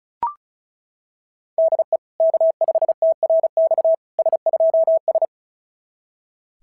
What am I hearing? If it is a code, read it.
Morse code, heard as a keyed tone, about 36 wpm, 690 Hz. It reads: DE K5TRX S2S
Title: SOTA QSO Practice 35wpm